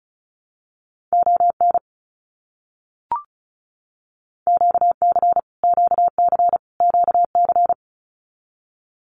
Morse code, heard as a keyed tone, about 34 wpm, 700 Hz. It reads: ON QC QC QC
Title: Canadian Provinces and Territories - Repeated 3x - 35wpm